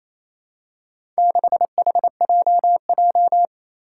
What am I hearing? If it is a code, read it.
Morse code, heard as a keyed tone, about 27 wpm, 700 Hz. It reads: 6HJJ
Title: Four Challenging Letter-Number Combinations 28wpm